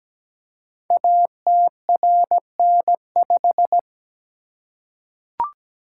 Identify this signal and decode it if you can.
Morse code, heard as a keyed tone, about 17 wpm, 700 Hz. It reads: ATRN5
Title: Code Groups - 5 Letter:Numbers - 17wpm